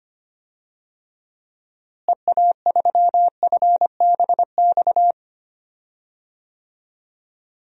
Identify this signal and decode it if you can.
Morse code, heard as a keyed tone, about 25 wpm, 700 Hz. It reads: EA3FBX
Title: DXCC-100 Call Signs - 6 Characters - Repeated 2x - 25wpm